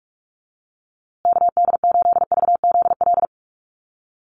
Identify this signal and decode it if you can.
Morse code, heard as a keyed tone, about 44 wpm, 690 Hz. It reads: KD8VZL
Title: USA Call Signs Repeated 3x 45wpm